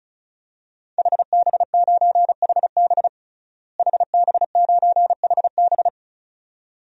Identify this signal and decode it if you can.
Morse code, heard as a keyed tone, about 36 wpm, 700 Hz. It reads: HB9HB HB9HB
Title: DXCC-100 Call Signs - 5 Characters - Repeated 2x - 35wpm